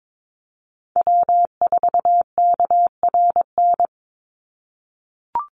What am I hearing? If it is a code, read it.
Morse code, heard as a keyed tone, about 22 wpm, 700 Hz. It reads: W4KRN
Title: NA SOTA Chaser Callsigns 22wpm